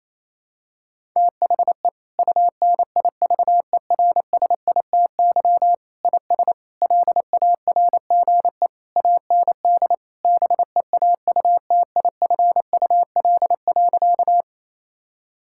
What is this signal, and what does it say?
Morse code, heard as a keyed tone, about 28 wpm, 700 Hz. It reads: THE UNIVERSITY IS LARGE AND BEAUTIFUL.
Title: Sentences from Top 1000 Words - Review Part 5 - 28wpm